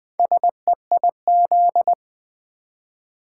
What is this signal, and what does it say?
Morse code, heard as a keyed tone, about 20 wpm, 700 Hz. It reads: SEIZ
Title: Build A Word - 1 Letter At A Time - 3 to 5-Letter Words - 20wpm